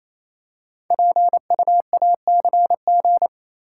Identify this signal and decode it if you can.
Morse code, heard as a keyed tone, about 27 wpm, 700 Hz. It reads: PUACG
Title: Code Groups - 5 Letters - 28wpm